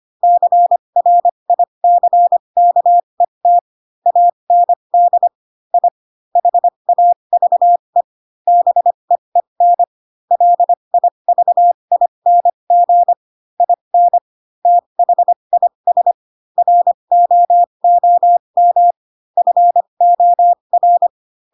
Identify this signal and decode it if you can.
Morse code, heard as a keyed tone, about 25 wpm, spaced as for 22 wpm, 700 Hz. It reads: CRICKET AND I HAVE BEEN LIVING IN THIS ROOM FOR
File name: Pinocchio / Pinocchio ch4 18 25 ik1hge